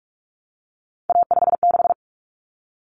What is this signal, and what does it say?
Morse code, heard as a keyed tone, about 44 wpm, 700 Hz. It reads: A56
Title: Common Cut Numbers - 3-Digit 45wpm